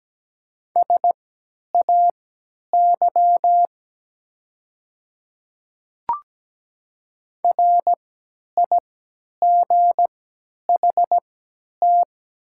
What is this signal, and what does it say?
Morse code, heard as a keyed tone, about 17 wpm, spaced as for 10 wpm, 700 Hz. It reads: SAY RIGHT
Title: 3x Character Spacing - Top 200 Words - 17wpm